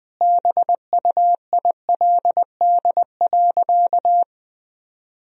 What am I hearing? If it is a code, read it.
Morse code, heard as a keyed tone, about 21 wpm, 700 Hz. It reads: BUILD.
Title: Sentences from Top 1000 Words - Review Part 1 - 20wpm